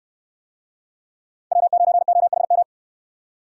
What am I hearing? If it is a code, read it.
Morse code, heard as a keyed tone, about 65 wpm, 700 Hz. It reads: W9OHM